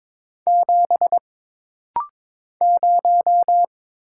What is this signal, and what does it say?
Morse code, heard as a keyed tone, about 22 wpm, 700 Hz. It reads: 7 0
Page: This rapid-fire timing is geared to push your ICR (Instant Character Recognition) skills.